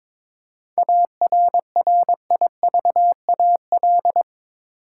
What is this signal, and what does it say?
Morse code, heard as a keyed tone, about 21 wpm, 700 Hz. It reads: ARRIVAL
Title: Seven Letter Words - 22wpm